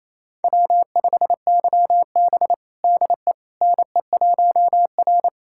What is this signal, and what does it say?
Morse code, heard as a keyed tone, about 28 wpm, 700 Hz. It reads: W5YB DE NE1R
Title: USA Callsign DE Callsign - 4 Characters - 28wpm